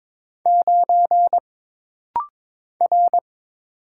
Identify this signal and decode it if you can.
Morse code, heard as a keyed tone, about 22 wpm, 700 Hz. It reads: 9 R
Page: This rapid-fire timing is geared to push your ICR (Instant Character Recognition) skills.